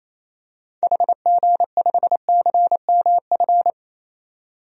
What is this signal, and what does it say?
Morse code, heard as a keyed tone, about 27 wpm, 700 Hz. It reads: HG5CMF
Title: DXCC-100 Call Signs - 6 Characters - 28wpm